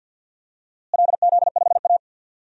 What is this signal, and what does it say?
Morse code, heard as a keyed tone, about 49 wpm, 700 Hz. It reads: LZ5A
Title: DXCC-100 Call Signs - 4 Characters - Repeated 2x - 50wpm